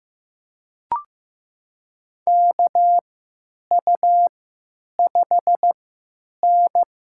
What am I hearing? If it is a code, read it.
Morse code, heard as a keyed tone, about 15 wpm, spaced as for 9 wpm, 700 Hz. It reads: KU5N
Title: USA Call Signs - 4 Characters - 3x Character Spacing - 15wpm